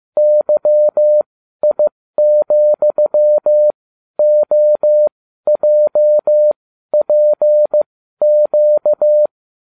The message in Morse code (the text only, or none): YI,OJPQ